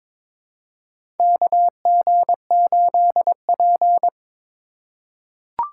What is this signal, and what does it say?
Morse code, heard as a keyed tone, about 22 wpm, 700 Hz. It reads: KG8P
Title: USA Call Signs - 4 Characters - 4x Character Spacing - 22wpm